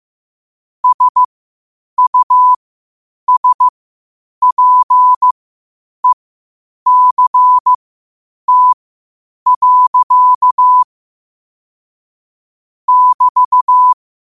Morse code, heard as a keyed tone, about 15 wpm, spaced as for 8 wpm, 1000 Hz. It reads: SUSPECT. <BT>
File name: qotdfiles / 8wpm